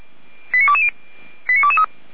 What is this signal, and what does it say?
Five-tone selective call (ZVEI), heard as quick runs of five tones, tones of about 70 ms. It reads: 89220 89223